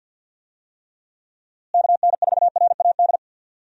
Morse code, heard as a keyed tone, about 52 wpm, 700 Hz. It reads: KN4RAD
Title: USA Call Signs - 6 Characters - Repeated 2x - 50wpm